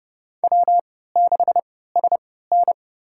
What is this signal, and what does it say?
Morse code, heard as a keyed tone, about 30 wpm, spaced as for 17 wpm, 700 Hz. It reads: W6SN
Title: USA Call Signs - 4 Characters - 3x Character Spacing - 30wpm